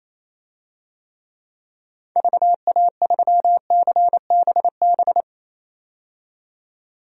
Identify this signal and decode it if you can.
Morse code, heard as a keyed tone, about 28 wpm, 700 Hz. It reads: VA3CBB
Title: DXCC-100 Call Signs - 6 Characters - 28wpm